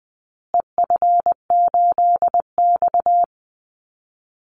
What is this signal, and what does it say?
Morse code, heard as a keyed tone, about 20 wpm, 700 Hz. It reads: EF8X
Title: Call Signs Repeated 2x - 20wpm